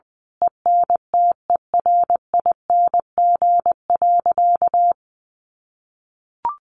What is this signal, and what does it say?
Morse code, heard as a keyed tone, about 20 wpm, 700 Hz. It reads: ENTERING.
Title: A Sound Of Thunder 20wpm